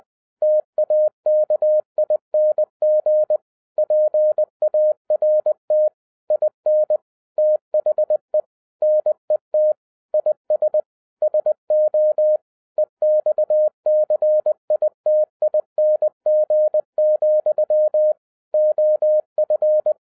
Morse code, heard as a keyed tone, about 20 wpm, 610 Hz. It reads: TAKING PART IN THE NET IS SO EXCITING, OF
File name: 2012-01-mp3 / n16-THROUGH EIGHT MONTHS-20wpm